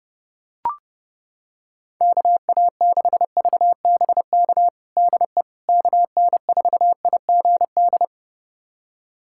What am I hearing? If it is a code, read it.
Morse code, heard as a keyed tone, about 32 wpm, 700 Hz. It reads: KA6VBK DE KN4IGD
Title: USA Callsign DE Callsign 30wpm